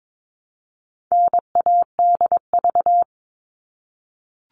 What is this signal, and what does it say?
Morse code, heard as a keyed tone, about 22 wpm, 700 Hz. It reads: NADV
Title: All Cut Numbers - 4-Digit 22wpm